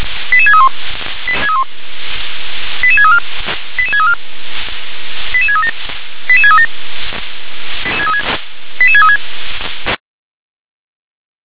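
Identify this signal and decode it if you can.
Five-tone selective call (ZVEI), heard as quick runs of five tones, tones of about 70 ms. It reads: 88631 88634 88634 88637 88637 88636
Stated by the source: ZVEI five-tone call received by SDR.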